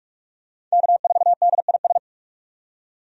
Morse code, heard as a keyed tone, about 46 wpm, 700 Hz. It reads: K4DIS
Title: USA Call Signs - 5 Characters - 45wpm